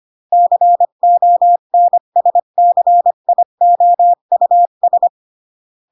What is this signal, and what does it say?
Morse code, heard as a keyed tone, about 25 wpm, 700 Hz. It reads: CONSCIOUS
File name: CW / jodorowsky / 82 commandments 25-20